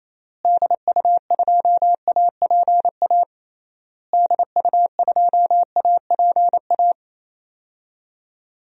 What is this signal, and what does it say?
Morse code, heard as a keyed tone, about 28 wpm, 700 Hz. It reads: DU2APA DU2APA
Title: DXCC-100 Call Signs - 6 Characters - Repeated 3x - 28wpm